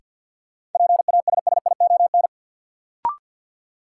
Morse code, heard as a keyed tone, about 48 wpm, 700 Hz. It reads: PASSION
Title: Sentences from Top 1300 Words - New Word Drill - 50wpm